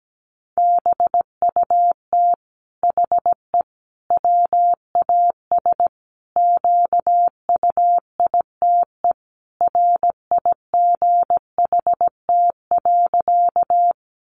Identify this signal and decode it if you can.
Morse code, heard as a keyed tone, about 17 wpm, 700 Hz. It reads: BUT HE WAS QUITE RIGHT.
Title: Sentences from Top 700 Words 17wpm